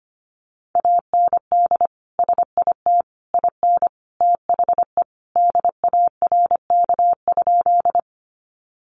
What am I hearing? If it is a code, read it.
Morse code, heard as a keyed tone, about 25 wpm, 700 Hz. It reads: AND SIT IN THE DARK?
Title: Sentences from Top 500 Words 25wpm